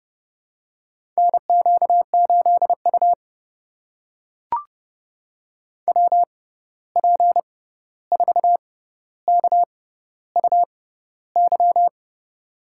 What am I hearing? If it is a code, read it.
Morse code, heard as a keyed tone, about 30 wpm, spaced as for 14 wpm, 700 Hz. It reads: NQ8U WP4KUY
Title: USA Call Signs - 6x Character Spacing - 30wpm